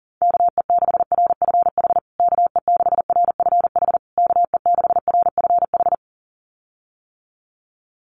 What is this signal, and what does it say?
Morse code, heard as a keyed tone, about 40 wpm, 700 Hz. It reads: KE6RFH KE6RFH KE6RFH
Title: USA Call Signs Repeated 3x 40wpm